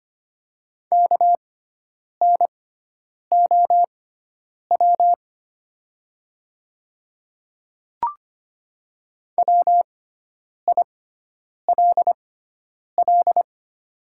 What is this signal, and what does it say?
Morse code, heard as a keyed tone, about 25 wpm, spaced as for 9 wpm, 700 Hz. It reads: KNOW WILL
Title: 6x Character Spacing - Top 100 Words - 25wpm